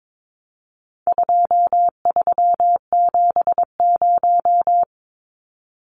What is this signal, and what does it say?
Morse code, heard as a keyed tone, about 22 wpm, 700 Hz. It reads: 2370
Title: Numbers - 4-Digit 22wpm